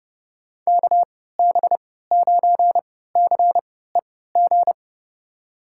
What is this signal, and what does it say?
Morse code, heard as a keyed tone, about 32 wpm, spaced as for 17 wpm, 700 Hz. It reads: KB9CEG
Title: USA Call Signs - 6 Characters - 3x Character Spacing - 30wpm